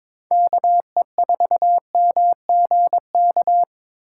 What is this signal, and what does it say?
Morse code, heard as a keyed tone, about 21 wpm, 700 Hz. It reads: KE4MGK
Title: USA Call Signs - 6 Characters - 3x Character Spacing - 22wpm